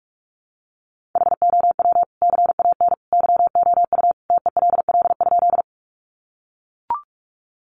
Morse code, heard as a keyed tone, about 44 wpm, 700 Hz. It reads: HOW CAN YOU TELL?
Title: Sentences from Top 200 Words 45wpm